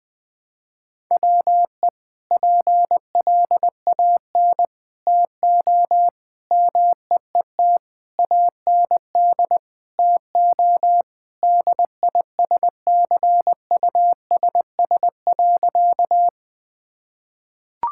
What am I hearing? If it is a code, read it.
Morse code, heard as a keyed tone, about 20 wpm, 700 Hz. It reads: WE PLAN TO MEET AND TO DISCUSS.
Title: Sentences with 2-Word Phrases - Spoken Beginning - 20wpm